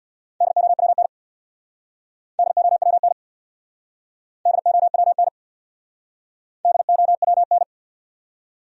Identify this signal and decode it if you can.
Morse code, heard as a keyed tone, about 66 wpm, 700 Hz. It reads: DOWN DOWN DOWN DOWN